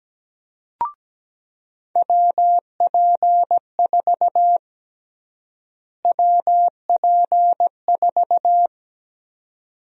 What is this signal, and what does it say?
Morse code, heard as a keyed tone, about 17 wpm, 700 Hz. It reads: WP4 WP4
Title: USA Call Sign Prefixes - Repeated 2x - 17wpm